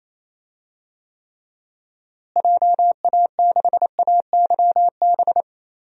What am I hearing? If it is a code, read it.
Morse code, heard as a keyed tone, about 28 wpm, 700 Hz. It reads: JA6AYB
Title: DXCC-100 Call Signs - 6 Characters - Repeated 2x - 28wpm